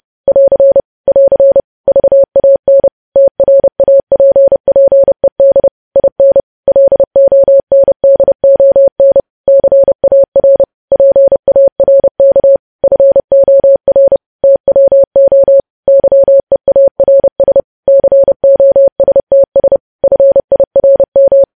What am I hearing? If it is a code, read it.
Morse code, heard as a keyed tone, about 30 wpm, 550 Hz. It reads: <AR> <AR> VAN TRAPPED IN LONDON CAR PARK FOR TWO YEARS COSTS FIRM
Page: news headlines in morse code at 30 wpm